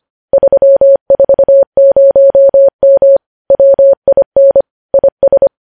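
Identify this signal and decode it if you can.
Morse code, heard as a keyed tone, about 25 wpm, 550 Hz. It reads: 340M WIN IS
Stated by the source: Morse code news headlines at 25 wpm.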